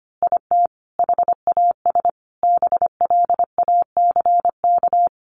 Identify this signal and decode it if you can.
Morse code, heard as a keyed tone, about 25 wpm, 700 Hz. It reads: IT HAS BLACK
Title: Easy Phrases 25wpm